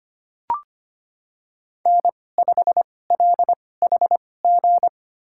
Morse code, heard as a keyed tone, about 25 wpm, spaced as for 18 wpm, 700 Hz. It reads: N5LHG
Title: USA Call Signs - 5 Characters - 2x Character Spacing - 25wpm